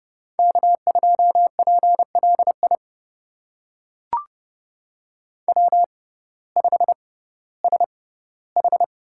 Morse code, heard as a keyed tone, about 30 wpm, spaced as for 16 wpm, 700 Hz. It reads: K2PLI W5SH
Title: USA Call Signs - 5 Characters - 6x Character Spacing - 30wpm